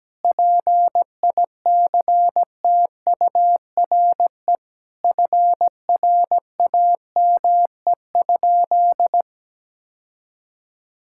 Morse code, heard as a keyed tone, about 17 wpm, 700 Hz. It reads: PICTURE FRAME?
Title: Sentences from Top 1000 Words - Review Part 6 - 17wpm